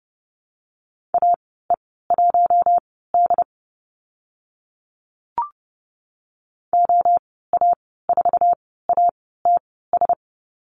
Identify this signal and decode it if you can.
Morse code, heard as a keyed tone, about 30 wpm, spaced as for 17 wpm, 700 Hz. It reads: AE1D OA4ATS
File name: Lesson 14g - Number 2 - Call Signs 30wpm